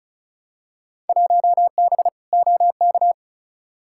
Morse code, heard as a keyed tone, about 34 wpm, 700 Hz. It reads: 1B OK
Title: Field Day Exchanges Encore - 35wpm